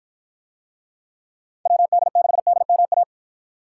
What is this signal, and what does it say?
Morse code, heard as a keyed tone, about 48 wpm, 700 Hz. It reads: WD6DMA